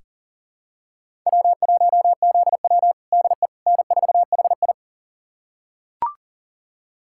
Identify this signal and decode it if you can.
Morse code, heard as a keyed tone, about 40 wpm, 700 Hz. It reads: W1ZW DE N4HI
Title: USA Callsign DE Callsign - 4 Characters - 40wpm